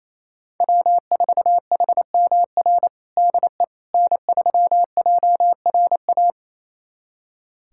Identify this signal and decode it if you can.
Morse code, heard as a keyed tone, about 27 wpm, 700 Hz. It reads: W4HMR DE N3JRA
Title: USA Callsign DE Callsign - 5 Characters - 28wpm